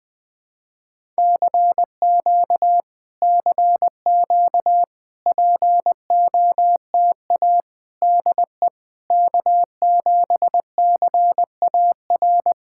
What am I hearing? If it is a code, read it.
Morse code, heard as a keyed tone, about 20 wpm, 690 Hz. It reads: CQ CQ POTA DE K7CAR
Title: POTA QSO Practice - 20wpm